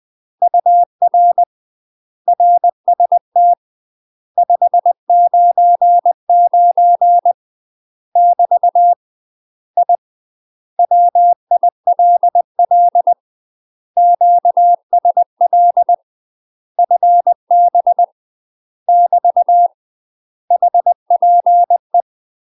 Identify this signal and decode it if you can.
Morse code, heard as a keyed tone, about 20 wpm, 700 Hz. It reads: UR RST 599 <BT> I WILL QSL FB <BT> HPE
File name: UK MASTER QSOs TRAINING/20WPM